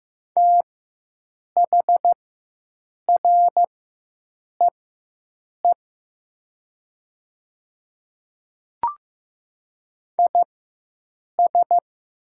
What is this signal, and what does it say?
Morse code, heard as a keyed tone, about 15 wpm, spaced as for 7 wpm, 700 Hz. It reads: THREE IS
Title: 4x Character Spacing - Top 200 Words - 15wpm